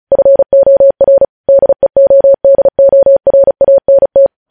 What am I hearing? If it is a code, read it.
Morse code, heard as a keyed tone, about 36 wpm, 550 Hz. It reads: FOR DEODORANT
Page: news headlines in morse code at 35 wpm